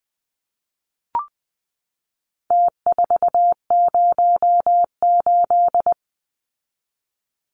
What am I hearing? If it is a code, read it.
Morse code, heard as a keyed tone, about 20 wpm, 700 Hz. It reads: T408
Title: Numbers - Contest Style - 4-Digit 20wpm